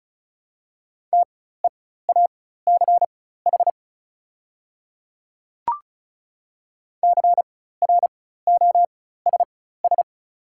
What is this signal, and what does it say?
Morse code, heard as a keyed tone, about 34 wpm, spaced as for 16 wpm, 700 Hz. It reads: TEACH CROSS
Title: 4x Character Spacing - Top 500 Words - 35wpm